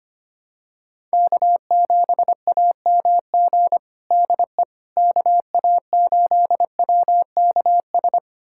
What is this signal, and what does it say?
Morse code, heard as a keyed tone, about 25 wpm, 700 Hz. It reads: K7AMG DE KA8WKS
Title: USA Callsign DE Callsign 25wpm